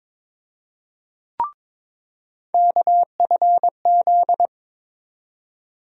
Morse code, heard as a keyed tone, about 21 wpm, 700 Hz. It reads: KFZ
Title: USA Call Sign Postfixes - 3 Letters 22wpm